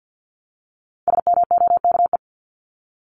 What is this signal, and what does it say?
Morse code, heard as a keyed tone, about 50 wpm, 700 Hz. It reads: SMOKE
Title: Top 1501-1750 Words 50wpm